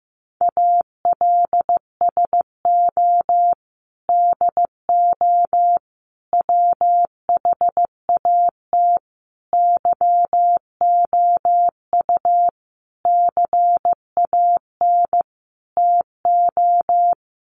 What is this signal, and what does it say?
Morse code, heard as a keyed tone, about 15 wpm, 700 Hz. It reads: ALSO DO WHAT YOU CAN TO
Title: Sentences from Top 1300 Words 15wpm